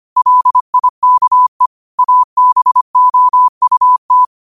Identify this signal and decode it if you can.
Morse code, heard as a keyed tone, about 25 wpm, 1000 Hz. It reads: LIKE ABOUT
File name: qotdfiles / 25wpm